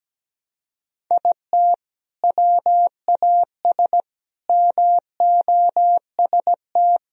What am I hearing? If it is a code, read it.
Morse code, heard as a keyed tone, about 17 wpm, 700 Hz. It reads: IT WAS MOST
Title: Sentences from Top 1000 Words - Review Part 1 - 17wpm